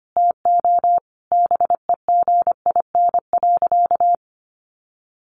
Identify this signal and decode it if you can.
Morse code, heard as a keyed tone, about 25 wpm, 700 Hz. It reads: TO BEGIN.
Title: Sentences from Top 1200 Words 25wpm